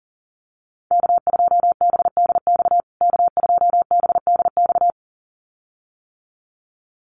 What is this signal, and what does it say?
Morse code, heard as a keyed tone, about 40 wpm, 700 Hz. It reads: K2BDX K2BDX
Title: USA Call Signs Repeated 2x 40wpm